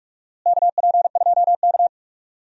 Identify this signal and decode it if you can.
Morse code, heard as a keyed tone, about 46 wpm, 700 Hz. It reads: KP2K
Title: DXCC-100 Call Signs - 4 Characters - Repeated 2x - 45wpm